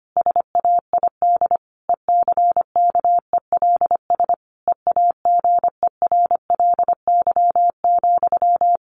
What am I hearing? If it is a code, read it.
Morse code, heard as a keyed tone, about 25 wpm, 700 Hz. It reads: SAID ECKELS EAGERLY,
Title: A Sound Of Thunder 25wpm